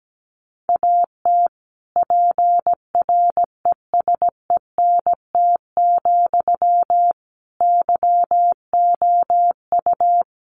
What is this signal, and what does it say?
Morse code, heard as a keyed tone, about 17 wpm, 700 Hz. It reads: AT PRESENT, YOU
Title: Sentences from Top 800 Words 17wpm